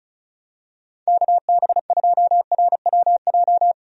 Morse code, heard as a keyed tone, about 36 wpm, 700 Hz. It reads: KB2RWJ
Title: USA Call Signs - 6 Characters - 35wpm